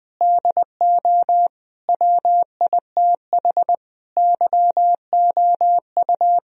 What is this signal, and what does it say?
Morse code, heard as a keyed tone, about 21 wpm, 700 Hz. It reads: DO WITH YOU
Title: Sentences from Top 1000 Words - Review Part 4 - 20wpm